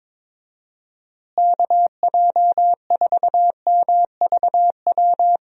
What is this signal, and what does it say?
Morse code, heard as a keyed tone, about 22 wpm, 700 Hz. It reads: KJ4MVW
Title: USA Call Signs - 6 Characters - Repeated 3x - 22wpm